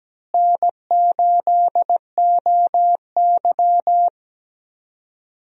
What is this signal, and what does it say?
Morse code, heard as a keyed tone, about 17 wpm, 700 Hz. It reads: N8OY
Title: USA Call Signs - 4 Characters - Repeated 2x - 17wpm